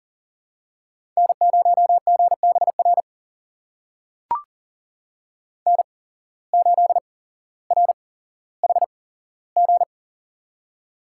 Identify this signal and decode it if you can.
Morse code, heard as a keyed tone, about 40 wpm, spaced as for 17 wpm, 700 Hz. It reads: N0GBR N8RHG
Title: USA Call Signs - 5 Characters - 8x Character Spacing - 40wpm